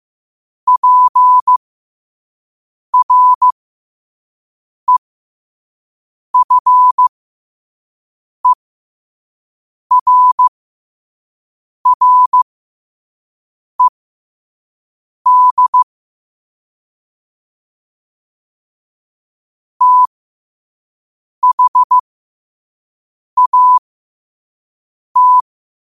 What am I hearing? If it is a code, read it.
Morse code, heard as a keyed tone, about 15 wpm, spaced as for 5 wpm, 1000 Hz. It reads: PREFERRED THAT
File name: qotdfiles / 5wpm